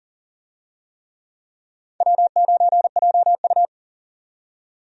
Morse code, heard as a keyed tone, about 40 wpm, 700 Hz. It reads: W9JU
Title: USA Call Signs - 4 Characters - 40wpm